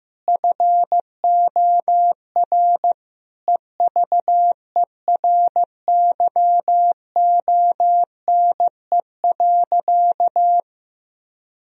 Morse code, heard as a keyed tone, about 15 wpm, 700 Hz. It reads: FOR EVERYONE.
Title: Sentences from Top 1000 Words - Review Part 4 - 15wpm